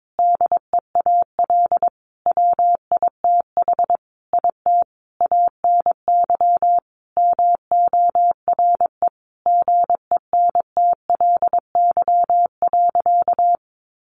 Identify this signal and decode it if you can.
Morse code, heard as a keyed tone, about 22 wpm, 700 Hz. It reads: DEAL WITH IT ANY MORE GENTLY.
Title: Sentences from Top 1500 Words 22wpm